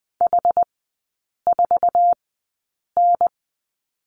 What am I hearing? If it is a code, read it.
Morse code, heard as a keyed tone, about 20 wpm, spaced as for 8 wpm, 700 Hz. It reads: H4N
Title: 107 Lesson 18e - Letter m - Three Characters 20wpm